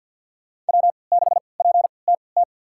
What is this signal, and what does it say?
Morse code, heard as a keyed tone, about 52 wpm, spaced as for 28 wpm, 700 Hz. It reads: W6PTT
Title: USA Call Signs - 5 Characters - 3x Character Spacing - 50wpm